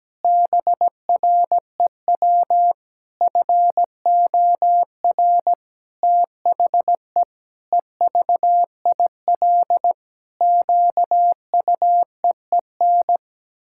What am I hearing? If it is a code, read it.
Morse code, heard as a keyed tone, about 17 wpm, 700 Hz. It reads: BREW FOR THE EVIL QUEEN
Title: Pangram Challenge v2 - 17wpm